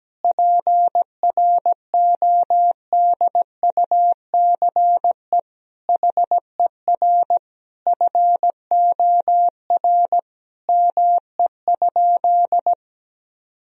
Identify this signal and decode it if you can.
Morse code, heard as a keyed tone, about 17 wpm, 700 Hz. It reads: PRODUCE HER FOR ME?
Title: Sentences from Top 1000 Words - Review Part 3 - 17wpm